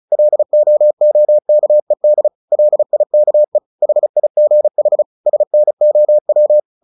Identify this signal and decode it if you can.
Morse code, heard as a keyed tone, about 34 wpm, 600 Hz. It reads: LOOKED LIKE HIGH SNOW
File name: the old man and the sea / the old man and the sea 1 35wpm